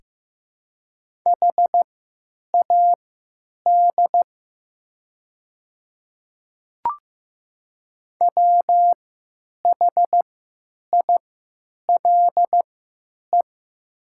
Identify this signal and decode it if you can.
Morse code, heard as a keyed tone, about 15 wpm, spaced as for 8 wpm, 700 Hz. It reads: HAD WHILE
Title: 3x Character Spacing - Top 300 Words - 15wpm